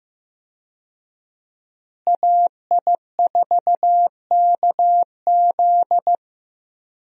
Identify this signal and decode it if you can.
Morse code, heard as a keyed tone, about 16 wpm, 700 Hz. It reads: AI4KZ
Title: USA Call Signs - 5 Characters - Repeated 3x - 15wpm